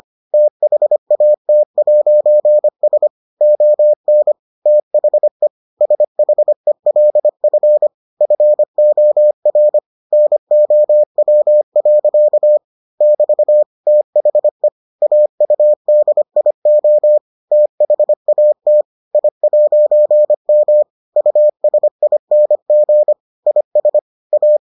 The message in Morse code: THAT'S ON THE SHELF FOR NOW. <BT> THE AUDIO THAT I'M USING IS A

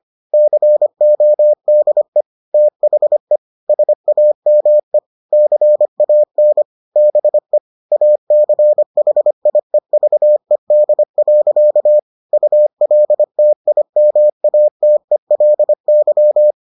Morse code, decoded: CODE THE SAME CAN BE ACHIEVED. ULTIMATELY

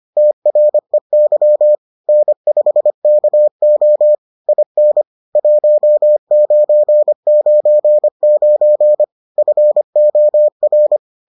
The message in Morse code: TREY N5KO IN 1999 FOR

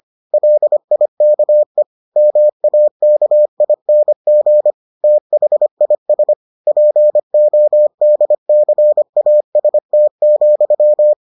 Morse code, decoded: LIKE MAKING THIS PODCAST,

